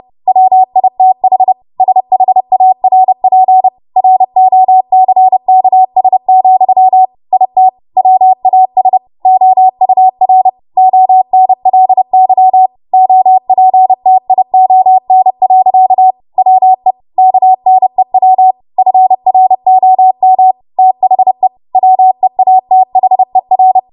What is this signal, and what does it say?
Morse code, heard as a keyed tone, about 30 wpm, 750 Hz. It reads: WITH SHARP ROCKS, IT WAS OUR ONLY OPTION. WE KNEW FROM THE WEATHER
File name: Morse / 30WPM